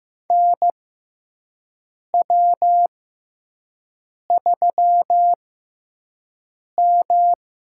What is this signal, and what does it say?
Morse code, heard as a keyed tone, about 15 wpm, spaced as for 5 wpm, 700 Hz. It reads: NW3M
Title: USA Call Signs - 4 Characters - 6x Character Spacing - 15wpm